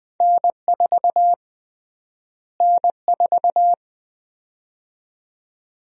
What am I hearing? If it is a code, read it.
Morse code, heard as a keyed tone, about 20 wpm, 700 Hz. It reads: N4 N4
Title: USA Call Sign Prefixes - Repeated 2x - 20wpm